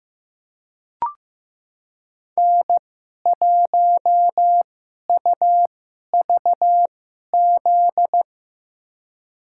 Morse code, heard as a keyed tone, about 15 wpm, spaced as for 11 wpm, 700 Hz. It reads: N1UVZ
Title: USA Call Signs - 2x Character Spacing - 15wpm